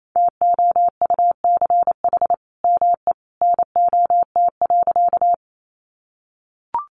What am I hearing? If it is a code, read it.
Morse code, heard as a keyed tone, about 28 wpm, 700 Hz. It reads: TOUCH ME NOT.
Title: Sentences from Top 1200 Words 28wpm